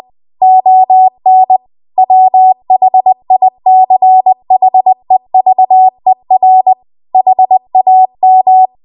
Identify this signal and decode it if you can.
Morse code, heard as a keyed tone, about 20 wpm, 750 Hz. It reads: ON WHICHEVER HAM
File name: Archive / 20 WPM / 20WPM